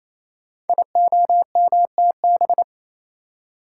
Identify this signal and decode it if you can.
Morse code, heard as a keyed tone, about 29 wpm, 700 Hz. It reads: IOMTB
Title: Code Groups - 5 Letters - 28wpm